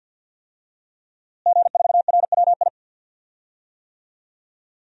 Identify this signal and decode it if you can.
Morse code, heard as a keyed tone, about 50 wpm, 700 Hz. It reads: G4RWI
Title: DXCC-100 Call Signs - 5 Characters - Repeated 2x - 50wpm